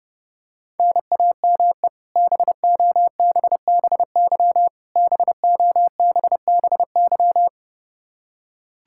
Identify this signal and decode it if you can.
Morse code, heard as a keyed tone, about 30 wpm, 700 Hz. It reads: NAME BOBBY BOBBY
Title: Practice QSO - Exchange 1 - Beginning - 30wpm